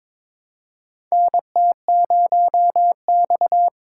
Morse code, heard as a keyed tone, about 22 wpm, 700 Hz. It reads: NT0X
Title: USA Call Signs - 4 Characters - 3x Character Spacing - 22wpm